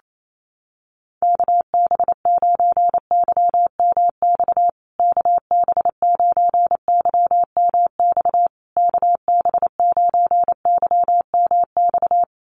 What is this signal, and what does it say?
Morse code, heard as a keyed tone, about 28 wpm, 700 Hz. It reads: KB9YMX KB9YMX KB9YMX
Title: USA Call Signs Repeated 3x 28wpm